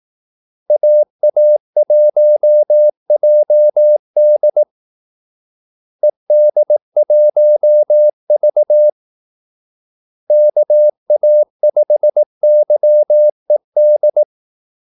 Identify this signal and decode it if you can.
Morse code, heard as a keyed tone, about 18 wpm, 600 Hz. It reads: AA1JD ED1V KA5YED
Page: Kaldesignaler 18 wpm